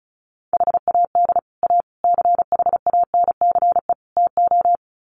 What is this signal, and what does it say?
Morse code, heard as a keyed tone, about 34 wpm, 700 Hz. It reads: HAD A CHANCE TO
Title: Sets of 4 Words 35wpm